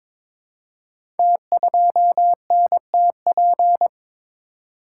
Morse code, heard as a keyed tone, about 23 wpm, 700 Hz. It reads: T2NTP
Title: Code Groups - 5 Letter:Numbers - 22wpm